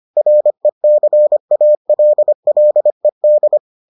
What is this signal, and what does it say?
Morse code, heard as a keyed tone, about 25 wpm, 600 Hz. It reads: RECALLED